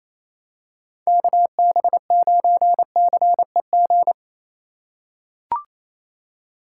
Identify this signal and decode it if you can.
Morse code, heard as a keyed tone, about 28 wpm, 700 Hz. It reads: KB9CEG
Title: USA Call Signs - 6 Characters - 8x Character Spacing - 28wpm